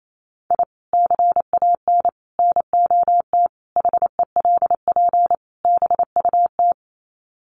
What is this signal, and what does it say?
Morse code, heard as a keyed tone, about 28 wpm, 700 Hz. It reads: I CAN NOT HELP BUT
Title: Sets of 5 Words Encore2 28wpm